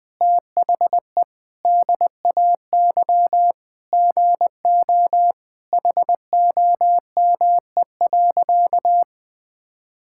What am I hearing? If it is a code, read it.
Morse code, heard as a keyed tone, about 20 wpm, 700 Hz. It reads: THE DAY GO HOME.
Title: Sentences with 3-Word Phrases - 20wpm